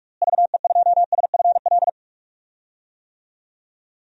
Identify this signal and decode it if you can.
Morse code, heard as a keyed tone, about 45 wpm, 700 Hz. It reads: VE2SFL
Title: DXCC-100 Call Signs - 6 Characters - Repeated 3x - 45wpm